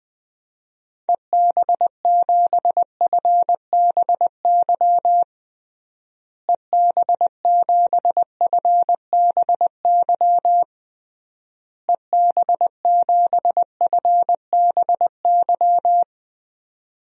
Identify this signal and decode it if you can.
Morse code, heard as a keyed tone, about 20 wpm, 700 Hz. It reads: EB7FBY EB7FBY EB7FBY
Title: DXCC-100 Call Signs - 6 Characters - Repeated 3x - 20wpm